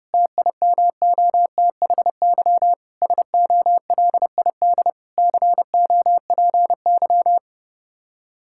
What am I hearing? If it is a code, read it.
Morse code, heard as a keyed tone, about 32 wpm, 700 Hz. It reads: TIMOTHY SOLID COPY
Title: K1USN SST QSO Practice 30wpm